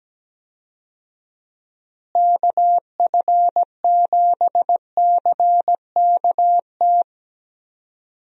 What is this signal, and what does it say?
Morse code, heard as a keyed tone, about 17 wpm, 700 Hz. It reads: KF7CKT
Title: USA Call Signs - 6 Characters - Repeated 2x - 17wpm